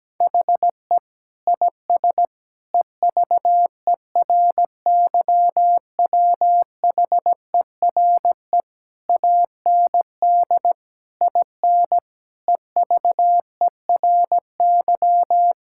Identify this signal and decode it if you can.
Morse code, heard as a keyed tone, about 17 wpm, 700 Hz. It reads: HE IS EVERYWHERE AND IN EVERY
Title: Sentences from Top 1600 Words 17wpm